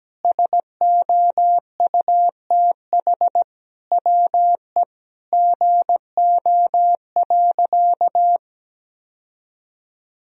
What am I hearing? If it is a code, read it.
Morse code, heard as a keyed tone, about 17 wpm, 700 Hz. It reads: SOUTH WE GO.
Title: Sentences from Top 1000 Words - Review Part 2 - 17wpm